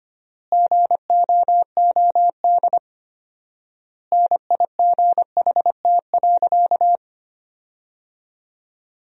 Morse code, heard as a keyed tone, about 25 wpm, 700 Hz. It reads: GOOD NIGHT.
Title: 4x Word-Spacing - Sentences from Top 300 Words - 25wpm